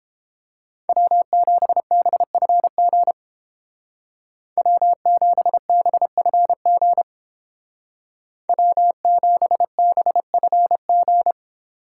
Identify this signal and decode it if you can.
Morse code, heard as a keyed tone, about 29 wpm, 700 Hz. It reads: W7BFG W7BFG W7BFG